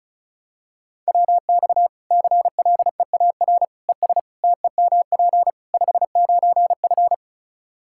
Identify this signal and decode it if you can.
Morse code, heard as a keyed tone, about 34 wpm, 700 Hz. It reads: WX CLEAR ES TEMP 59F
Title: Practice QSO - Exchange 2 - Info - 35wpm